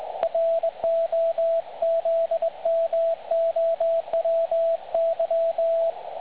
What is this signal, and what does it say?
Morse code, heard as a keyed tone, about 20 wpm, 660 Hz, with radio noise. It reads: ROZMOWY
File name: dlazaawansowanych / nagraniazpasma / websdr recording 3565.0kHz